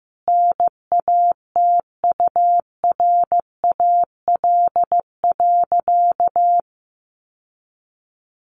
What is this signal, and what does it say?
Morse code, heard as a keyed tone, about 15 wpm, 700 Hz. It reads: NATURAL.
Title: Sentences from Top 700 Words 15wpm